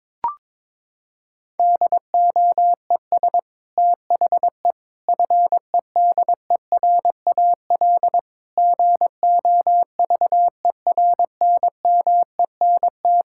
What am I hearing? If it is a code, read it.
Morse code, heard as a keyed tone, about 22 wpm, 700 Hz. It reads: DOES THE FEDERAL GOVERNMENT
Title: Sentences from Top 1000 Words - Review Part 5 - 22wpm